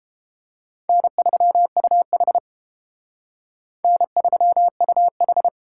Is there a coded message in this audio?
Yes: Morse code from 0.9 to 5.5 s, heard as a keyed tone, about 30 wpm, 700 Hz. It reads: N3UH N3UH